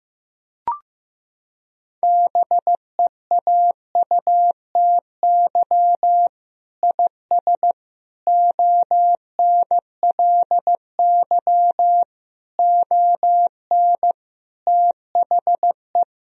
Morse code, heard as a keyed tone, about 15 wpm, 700 Hz. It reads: BEAUTY IS ONLY ON THE
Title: Sentences from Top 1000 Words - Review Part 3 - 15wpm